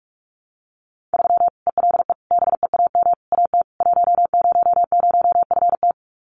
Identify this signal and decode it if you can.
Morse code, heard as a keyed tone, about 44 wpm, 700 Hz. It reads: 3 ELE BEAM AT 100FT
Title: Common QSO Phrases 45wpm